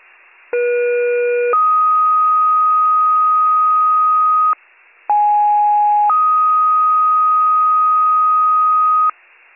Two-tone page, heard as a one-second tone, then a three-second tone, about 480 then 1250 Hz, about 830 then 1250 Hz. It is heard from 0.5 to 9.1 s.